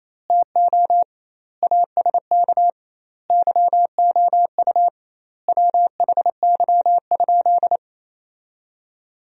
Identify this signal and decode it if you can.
Morse code, heard as a keyed tone, about 28 wpm, 700 Hz. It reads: TO ASK YOU WHY?
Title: 2x Word-Spacing - Sentences from Top 200 Words - 28wpm